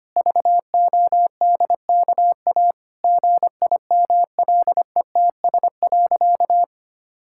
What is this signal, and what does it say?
Morse code, heard as a keyed tone, about 25 wpm, 700 Hz. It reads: VODKA GIMLETS.
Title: Pangram Challenge v2 - 25wpm